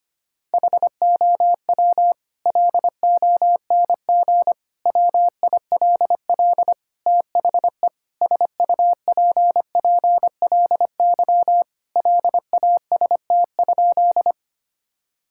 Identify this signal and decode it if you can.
Morse code, heard as a keyed tone, about 25 wpm, 700 Hz. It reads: HOW LONG WILL THE SUPPLY LAST?
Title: Sentences from Top 1000 Words - Review Part 5 - 25wpm